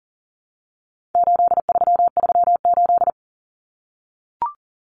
Morse code, heard as a keyed tone, about 40 wpm, 700 Hz. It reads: 8338
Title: Numbers - 4-Digit 40wpm